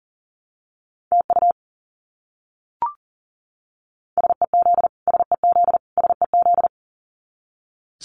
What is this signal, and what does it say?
Morse code, heard as a keyed tone, about 40 wpm, 700 Hz. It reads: TU SEZ SEZ SEZ
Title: Abbreviations Repeated 3x - 40wpm